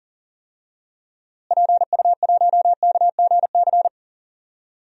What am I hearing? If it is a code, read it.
Morse code, heard as a keyed tone, about 40 wpm, 700 Hz. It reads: PU1KGC
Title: DXCC-100 Call Signs - 6 Characters - Repeated 3x - 40wpm